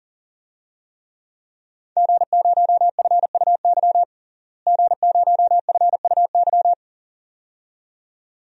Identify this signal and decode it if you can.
Morse code, heard as a keyed tone, about 40 wpm, 700 Hz. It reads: G0FUY G0FUY
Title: DXCC-100 Call Signs - 5 Characters - Repeated 2x - 40wpm